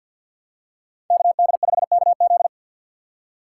Morse code, heard as a keyed tone, about 53 wpm, 700 Hz. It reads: KD5KZ
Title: USA Call Signs - 5 Characters - 2x Character Spacing - 50wpm